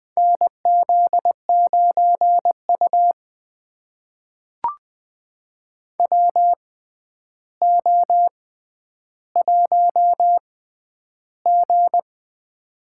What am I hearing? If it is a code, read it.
Morse code, heard as a keyed tone, about 20 wpm, spaced as for 10 wpm, 700 Hz. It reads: NZ9U WO1G
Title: USA Call Signs - 4 Characters - 6x Character Spacing - 20wpm